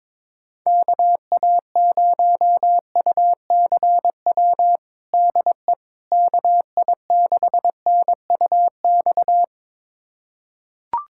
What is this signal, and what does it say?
Morse code, heard as a keyed tone, about 22 wpm, 700 Hz. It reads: KA0UCW DE KI6NUX
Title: USA Callsign DE Callsign - 6 Characters - 22wpm